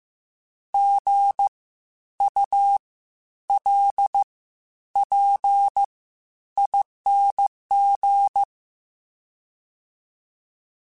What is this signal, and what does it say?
Morse code, heard as a keyed tone, about 15 wpm, spaced as for 10 wpm, 800 Hz. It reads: GULPING
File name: Spaced Alphas / Spaced Alphas mp3s / Gs sp @13